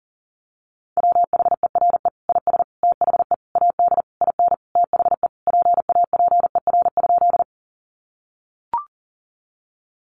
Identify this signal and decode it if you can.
Morse code, heard as a keyed tone, about 40 wpm, 700 Hz. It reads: WHERE IS THE AD IN THE PAPER?
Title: Sentences from Top 1300 Words 40wpm